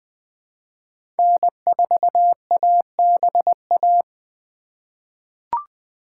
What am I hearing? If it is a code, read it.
Morse code, heard as a keyed tone, about 20 wpm, 700 Hz. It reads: N4ABA
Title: USA Call Signs - 5 Characters - 4x Character Spacing - 20wpm